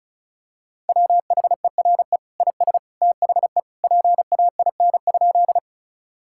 Morse code, heard as a keyed tone, about 34 wpm, 700 Hz. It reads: WHERE IS THE PAIN?
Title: Sentences from Top 1600 Words 35wpm